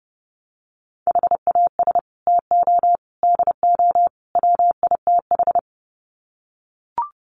Morse code, heard as a keyed tone, about 30 wpm, 700 Hz. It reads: HAS TO DO WITH
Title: Sets of 4 Words 30wpm